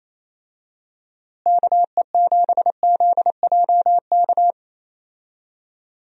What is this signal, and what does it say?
Morse code, heard as a keyed tone, about 28 wpm, 700 Hz. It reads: KE7ZJK
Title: USA Call Signs - 6 Characters - Repeated 2x - 28wpm